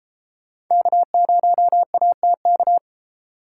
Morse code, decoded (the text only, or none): K0ATK